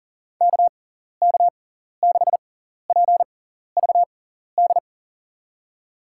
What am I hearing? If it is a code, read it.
Morse code, heard as a keyed tone, about 40 wpm, spaced as for 14 wpm, 700 Hz. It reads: KK6PVD
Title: USA Call Signs - 6 Characters - 6x Character Spacing - 40wpm